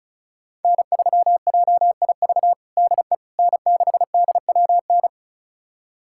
Morse code, heard as a keyed tone, about 36 wpm, 700 Hz. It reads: N3JIV DE N6DWN
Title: USA Callsign DE Callsign - 5 Characters - 35wpm